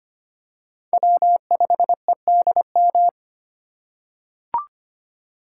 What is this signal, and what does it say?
Morse code, heard as a keyed tone, about 25 wpm, 700 Hz. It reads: W5EDM
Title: USA Call Signs - 5 Characters - 2x Character Spacing - 25wpm